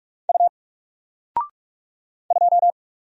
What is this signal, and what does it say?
Morse code, heard as a keyed tone, about 44 wpm, 700 Hz. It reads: U 2
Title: Single Letter-Number 45wpm